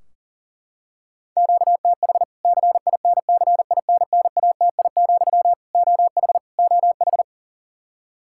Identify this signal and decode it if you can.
Morse code, heard as a keyed tone, about 40 wpm, 700 Hz. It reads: QTH CINCINNATI, OH OH
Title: K1USN SST QSO Practice 40wpm